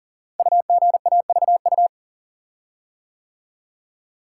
Morse code, heard as a keyed tone, about 42 wpm, 690 Hz. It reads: UGAVU
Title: Code Groups - 5 Letter:Numbers - 40wpm